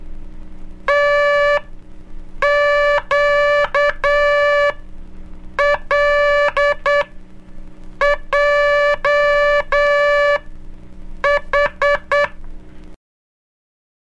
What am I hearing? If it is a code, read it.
Morse code, heard as a keyed tone, about 9 wpm, spaced as for 6 wpm, 580 Hz. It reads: TQLJH
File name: cursos / Curso CW mp3 / CW Aula - 04